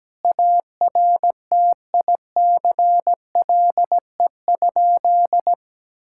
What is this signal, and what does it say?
Morse code, heard as a keyed tone, about 17 wpm, 700 Hz. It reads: ARTICLE?
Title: Sentences from Top 1000 Words - Review Part 5 - 17wpm